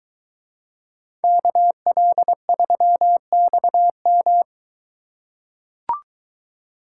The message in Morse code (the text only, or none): KL3XM